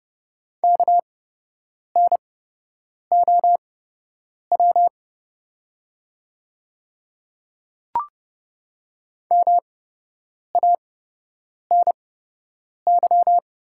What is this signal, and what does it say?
Morse code, heard as a keyed tone, about 31 wpm, spaced as for 8 wpm, 700 Hz. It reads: KNOW MANY
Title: 8x Character Spacing - Top 100 Words - 30wpm